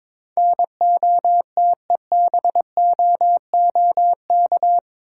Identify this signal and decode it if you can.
Morse code, heard as a keyed tone, about 21 wpm, 700 Hz. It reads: NOTEBOOK
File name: practice / 22wpm / Eight Letter Words - 22wpm